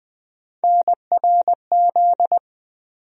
Morse code, heard as a keyed tone, about 20 wpm, 700 Hz. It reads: NRZ
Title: USA Call Sign Postfixes - 3 Letters 20wpm